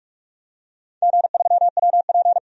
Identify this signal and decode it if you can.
Morse code, heard as a keyed tone, about 46 wpm, 700 Hz. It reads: G3WP
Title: DXCC-100 Call Signs - 4 Characters - 45wpm